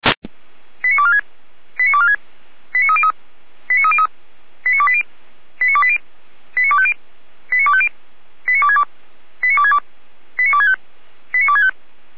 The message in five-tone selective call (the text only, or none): 89246 89246 89303 89303 89280 89280 89250 89250 89262 89262 89265 89265